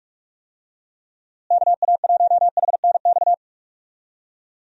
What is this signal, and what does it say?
Morse code, heard as a keyed tone, about 46 wpm, 700 Hz. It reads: KA1HNX
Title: USA Call Signs - 6 Characters - Repeated 2x - 45wpm